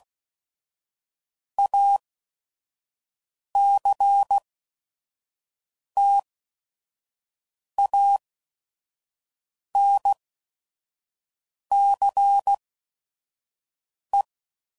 Morse code, heard as a keyed tone, about 16 wpm, spaced as for 5 wpm, 800 Hz. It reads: ACTANCE
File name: CW Code Groups @5WPM / Technical Subjects / XL